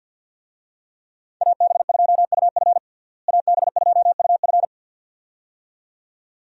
Morse code, heard as a keyed tone, about 51 wpm, 700 Hz. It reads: AB2UF AB2UF
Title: USA Call Signs - 5 Characters - Repeated 2x - 50wpm